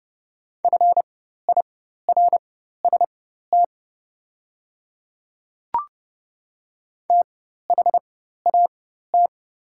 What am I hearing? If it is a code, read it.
Morse code, heard as a keyed tone, about 30 wpm, spaced as for 14 wpm, 700 Hz. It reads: FIRST THAT
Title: 4x Character Spacing - Top 100 Words - 30wpm